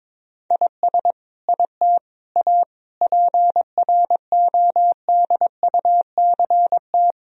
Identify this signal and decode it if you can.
Morse code, heard as a keyed tone, about 22 wpm, 700 Hz. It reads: IS IT A PRODUCT
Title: Sentences from Top 1000 Words - Review Part 2 - 22wpm